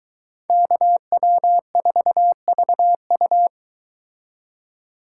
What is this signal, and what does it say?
Morse code, heard as a keyed tone, about 23 wpm, 700 Hz. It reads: KW4VU